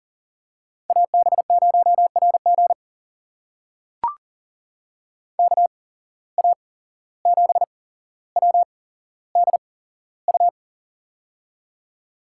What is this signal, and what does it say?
Morse code, heard as a keyed tone, about 40 wpm, spaced as for 16 wpm, 700 Hz. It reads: AB0RG KA7WDU
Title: USA Call Signs - 8x Character Spacing - 40wpm